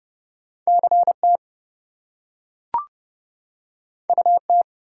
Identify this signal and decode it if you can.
Morse code, heard as a keyed tone, about 31 wpm, 700 Hz. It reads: CT UT
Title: US States and Canadian Provinces 30wpm